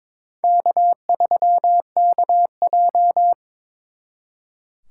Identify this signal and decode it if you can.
Morse code, heard as a keyed tone, about 26 wpm, 700 Hz. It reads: K3KJ
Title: Speed Racing - USA 1x2 Call Signs 22wpm